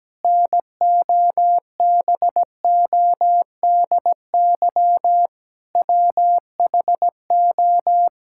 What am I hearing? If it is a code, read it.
Morse code, heard as a keyed tone, about 17 wpm, 700 Hz. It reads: NOBODY WHO
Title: Sentences from Top 1600 Words 17wpm